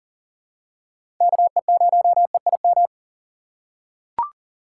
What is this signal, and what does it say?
Morse code, heard as a keyed tone, about 40 wpm, 700 Hz. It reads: KE0EIM
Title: USA Call Signs - 4x Character Spacing - 40wpm